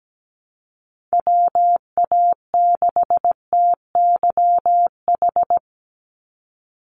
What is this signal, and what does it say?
Morse code, heard as a keyed tone, about 17 wpm, 700 Hz. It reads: WA6TYH
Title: USA Call Signs 17wpm